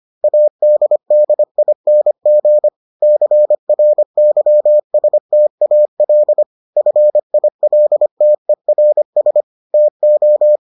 Morse code, heard as a keyed tone, about 25 wpm, 600 Hz. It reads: ADDING CRYSTAL FILTERS TO